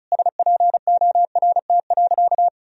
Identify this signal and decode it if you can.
Morse code, heard as a keyed tone, about 36 wpm, 700 Hz. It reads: SPORT.
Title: Sentences from Top 1000 Words - Review Part 4 - 35wpm